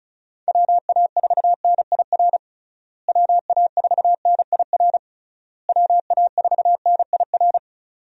Morse code, heard as a keyed tone, about 34 wpm, 700 Hz. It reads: WA4NIR WA4NIR WA4NIR
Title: USA Call Signs - 6 Characters - Repeated 3x - 35wpm